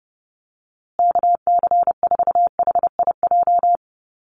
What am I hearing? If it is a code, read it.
Morse code, heard as a keyed tone, about 30 wpm, 700 Hz. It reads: KC4HIJ
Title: USA Call Signs Repeated 2x 30wpm